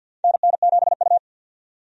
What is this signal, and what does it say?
Morse code, heard as a keyed tone, about 51 wpm, 700 Hz. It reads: NN7U
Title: USA Call Signs - 4 Characters - 50wpm